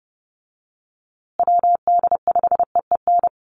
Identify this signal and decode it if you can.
Morse code, heard as a keyed tone, about 28 wpm, 700 Hz. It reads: WD5EEN
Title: USA Call Signs Repeated 3x 30wpm